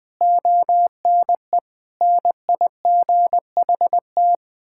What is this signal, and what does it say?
Morse code, heard as a keyed tone, about 21 wpm, 700 Hz. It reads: ONE NIGHT
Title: Sentences from Top 1600 Words 20wpm